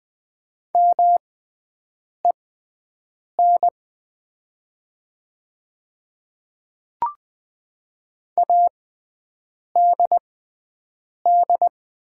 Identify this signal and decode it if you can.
Morse code, heard as a keyed tone, about 20 wpm, spaced as for 7 wpm, 700 Hz. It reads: MEN ADD
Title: 6x Character Spacing - Top 200 Words - 20wpm